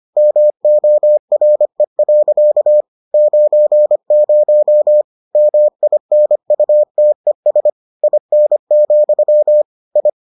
Morse code, heard as a keyed tone, about 25 wpm, 600 Hz. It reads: MORE. 90 MINUTES IN, I